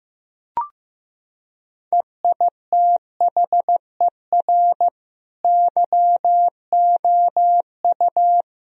Morse code, heard as a keyed tone, about 16 wpm, 700 Hz. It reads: EITHER YOU
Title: Sentences from Top 1000 Words - Review Part 4 - 15wpm